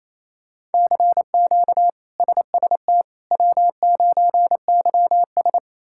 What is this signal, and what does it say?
Morse code, heard as a keyed tone, about 28 wpm, 700 Hz. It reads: CQ SST W9YS
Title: K1USN SST QSO Practice - 28wpm